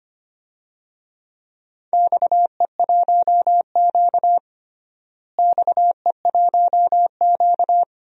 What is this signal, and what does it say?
Morse code, heard as a keyed tone, about 25 wpm, 700 Hz. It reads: XE1Q XE1Q
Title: DXCC-100 Call Signs - 4 Characters - Repeated 2x - 25wpm